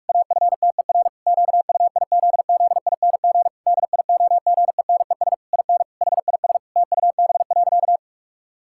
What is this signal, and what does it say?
Morse code, heard as a keyed tone, about 46 wpm, 700 Hz. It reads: AFTER QUIZZING DIOGENES IN HIS TUB.
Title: Pangram Challenge v2 - 45wpm